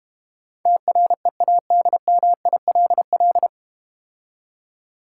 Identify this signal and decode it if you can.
Morse code, heard as a keyed tone, about 32 wpm, 700 Hz. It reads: TREADMILL